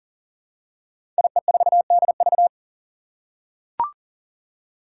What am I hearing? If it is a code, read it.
Morse code, heard as a keyed tone, about 40 wpm, 700 Hz. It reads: IE4DV
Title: Five Challenging Letter-Number Combinations 40wpm